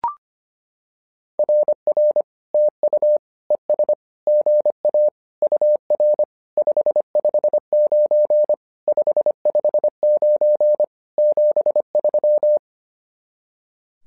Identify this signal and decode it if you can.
Morse code, heard as a keyed tone, about 25 wpm, 600 Hz. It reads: RR TU ES GA UR 559 559 73
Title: SOTA QSO Practice 25wpm